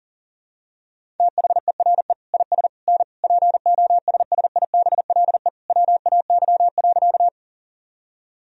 Morse code, heard as a keyed tone, about 40 wpm, 700 Hz. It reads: THERE IS N POSSIBLE WAY.
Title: Sentences from Top 500 Words and Abbreviations 40wpm